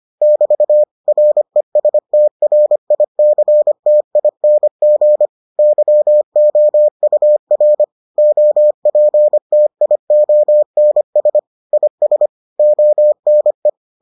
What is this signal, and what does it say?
Morse code, heard as a keyed tone, about 25 wpm, 600 Hz. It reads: <BT> RESTRICTING YOUR OPTIONS IS ONE